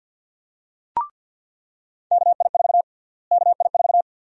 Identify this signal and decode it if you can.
Morse code, heard as a keyed tone, about 48 wpm, 700 Hz. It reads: KI4 KI4
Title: USA Call Sign Prefixes - Repeated 2x - 50wpm